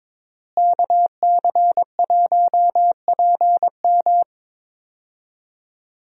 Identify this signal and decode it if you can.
Morse code, heard as a keyed tone, about 22 wpm, 700 Hz. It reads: KC1PM
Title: USA Call Signs - 5 Characters - Repeated 2x - 22wpm